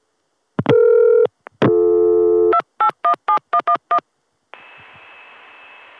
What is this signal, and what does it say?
DTMF, heard as telephone touch tones, tones of about 90 ms. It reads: 2817111